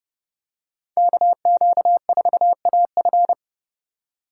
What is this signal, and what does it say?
Morse code, heard as a keyed tone, about 29 wpm, 700 Hz. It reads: KQ4AF